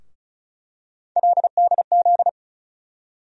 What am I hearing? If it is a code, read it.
Morse code, heard as a keyed tone, about 36 wpm, 700 Hz. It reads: LDZ
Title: USA Call Sign Postfixes - 3 Letters 35wpm